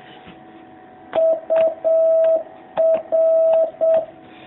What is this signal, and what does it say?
Morse code, heard as a keyed tone, about 7 wpm, 660 Hz. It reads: UR
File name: jh8cbh / 2019cw